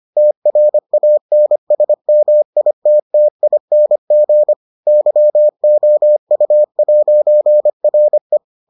Morse code, heard as a keyed tone, about 25 wpm, 600 Hz. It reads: TRANSMITTING YOU'RE